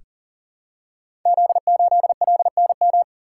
Morse code, heard as a keyed tone, about 38 wpm, 690 Hz. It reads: Z8LNM
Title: Code Groups - 5 Letter:Numbers - 40wpm